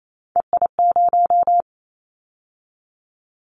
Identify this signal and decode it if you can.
Morse code, heard as a keyed tone, about 28 wpm, 700 Hz. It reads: EI0
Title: Callsign Prefixes for Top 50 DXCC - 28wpm